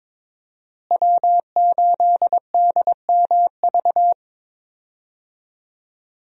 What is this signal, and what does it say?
Morse code, heard as a keyed tone, about 22 wpm, 700 Hz. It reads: W8DMV
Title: USA Call Signs - 5 Characters - Repeated 2x - 22wpm